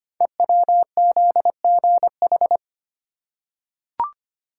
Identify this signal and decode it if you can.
Morse code, heard as a keyed tone, about 25 wpm, 700 Hz. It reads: EWZGH
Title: Code Groups - 5 Letters - 25wpm